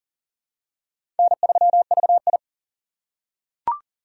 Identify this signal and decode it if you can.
Morse code, heard as a keyed tone, about 40 wpm, 700 Hz. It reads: N3VI
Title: USA Call Signs - 4 Characters - 4x Character Spacing - 40wpm